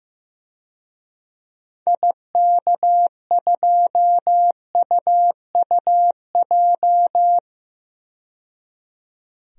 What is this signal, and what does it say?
Morse code, heard as a keyed tone, about 15 wpm, 700 Hz. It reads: IK2UUJ
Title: DXCC-100 Call Signs - 6 Characters - 15wpm